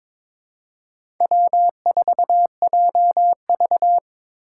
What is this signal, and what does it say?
Morse code, heard as a keyed tone, about 21 wpm, 700 Hz. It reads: W4JV
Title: USA Call Signs - 4 Characters - 2x Character Spacing - 22wpm